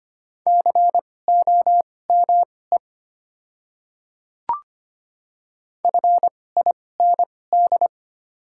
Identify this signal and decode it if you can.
Morse code, heard as a keyed tone, about 25 wpm, spaced as for 18 wpm, 700 Hz. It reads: COME FIND
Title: 2x Character Spacing - Top 300 Words - 25wpm